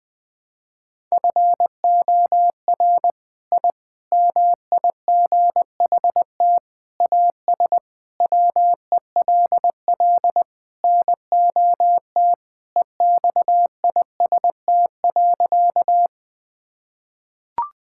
Morse code, heard as a keyed tone, about 21 wpm, 700 Hz. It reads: FOR I MIGHT AS WELL NOT EXIST.
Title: Sentences from Top 1000 Words - Review Part 5 - 20wpm